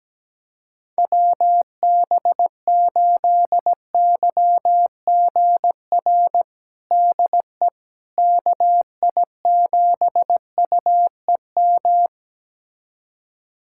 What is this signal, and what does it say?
Morse code, heard as a keyed tone, about 17 wpm, 700 Hz. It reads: WB8YGR DE KI7UEM
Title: USA Callsign DE Callsign - 6 Characters - 17wpm